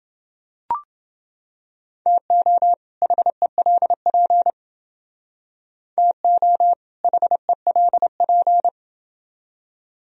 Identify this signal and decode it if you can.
Morse code, heard as a keyed tone, about 29 wpm, 700 Hz. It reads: TO HELP TO HELP